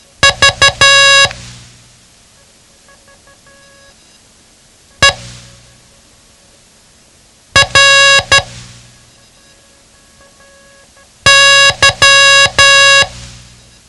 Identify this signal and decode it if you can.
Morse code, heard as a keyed tone, about 12 wpm, spaced as for 3 wpm, 1650 Hz. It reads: VERY
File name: ham radio / CodeQuick / Code Quick 07